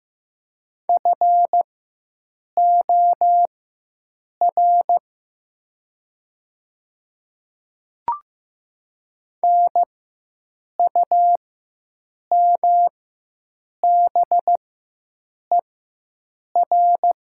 Morse code, heard as a keyed tone, about 15 wpm, spaced as for 7 wpm, 700 Hz. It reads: FOR NUMBER
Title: 4x Character Spacing - Top 200 Words - 15wpm